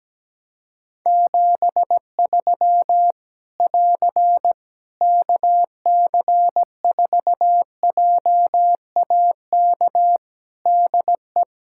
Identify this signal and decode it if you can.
Morse code, heard as a keyed tone, about 17 wpm, 700 Hz. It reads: 73 <AR> KC4JAK DE
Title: Practice QSO - Exchange 4 - Ending - 17wpm